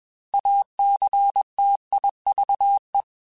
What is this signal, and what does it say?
Morse code, heard as a keyed tone, about 21 wpm, 800 Hz. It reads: ACTIVE
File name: Technical Subjects / Technical Subjects-mp3s / AND Gate @20